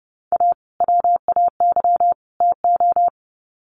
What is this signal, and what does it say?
Morse code, heard as a keyed tone, about 30 wpm, 700 Hz. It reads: A WAY TO
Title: Sets of 3 Words 30wpm